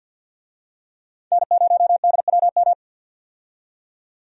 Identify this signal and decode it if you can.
Morse code, heard as a keyed tone, about 47 wpm, 700 Hz. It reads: N0DWM
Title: USA Call Signs - 5 Characters - 50wpm